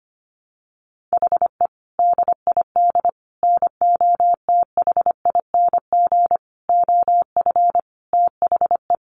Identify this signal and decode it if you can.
Morse code, heard as a keyed tone, about 25 wpm, 700 Hz. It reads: HE DID NOTHING OF THE
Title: Sentences from Top 1300 Words 25wpm